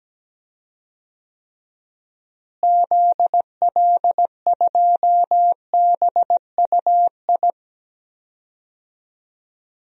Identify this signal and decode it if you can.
Morse code, heard as a keyed tone, about 17 wpm, 700 Hz. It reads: ZL2BUI
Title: DXCC-100 Call Signs - 6 Characters - Repeated 2x - 17wpm